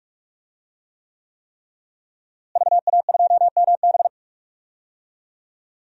Morse code, heard as a keyed tone, about 41 wpm, 700 Hz. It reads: VA2MB
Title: DXCC-100 Call Signs - 5 Characters - Repeated 3x - 45wpm